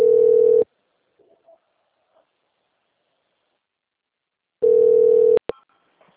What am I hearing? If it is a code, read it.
Ringback tone (North America), 440 + 480 Hz.
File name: out- - - - - .6191